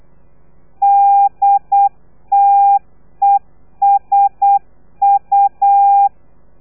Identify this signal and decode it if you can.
Morse code, heard as a keyed tone, about 8 wpm, 800 Hz. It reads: DTESU
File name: telegrafia / i2tfj / Lez3 35